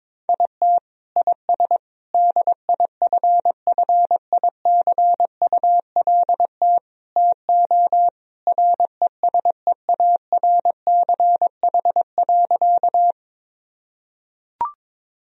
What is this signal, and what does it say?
Morse code, heard as a keyed tone, about 21 wpm, 700 Hz. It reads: IT IS DIFFICULT TO RESEARCH.
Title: Sentences from Top 1000 Words - Review Part 4 - 22wpm